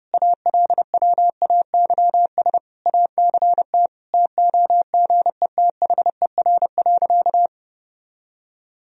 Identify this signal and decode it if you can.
Morse code, heard as a keyed tone, about 32 wpm, 700 Hz. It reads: ALWAYS ACT TOGETHER.
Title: Sentences from Top 1000 Words - Review Part 1 - 30wpm